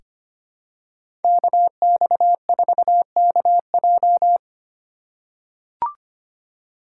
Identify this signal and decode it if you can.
Morse code, heard as a keyed tone, about 25 wpm, 700 Hz. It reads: KX4KJ
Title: USA Call Signs - 2x Character Spacing - 25wpm